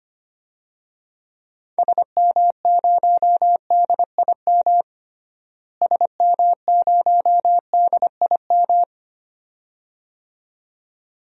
Morse code, heard as a keyed tone, about 25 wpm, 700 Hz. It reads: SM0DIM SM0DIM
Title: DXCC-100 Call Signs - 6 Characters - Repeated 2x - 25wpm